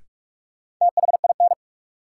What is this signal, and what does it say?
Morse code, heard as a keyed tone, about 46 wpm, 700 Hz. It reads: THIN
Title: Sentences from Top 1200 Words - New Word Drill Alt - 45wpm